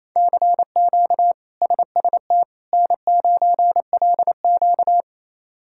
Morse code, heard as a keyed tone, about 27 wpm, 700 Hz. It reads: CQ SST N9LQ
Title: K1USN SST QSO Practice - 28wpm